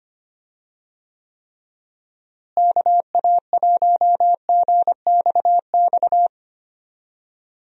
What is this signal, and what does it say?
Morse code, heard as a keyed tone, about 25 wpm, 700 Hz. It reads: KA1GXX
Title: USA Call Signs - 6 Characters - Repeated 2x - 25wpm